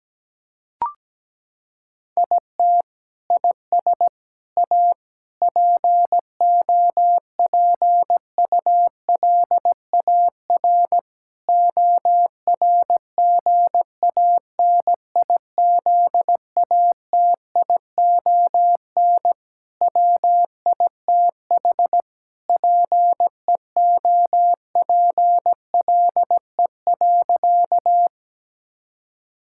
Morse code, heard as a keyed tone, about 17 wpm, 700 Hz. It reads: IT IS A POPULAR ORGANIZATION WITH PEOPLE.
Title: Sentences from Top 1000 Words - Review Part 5 - 17wpm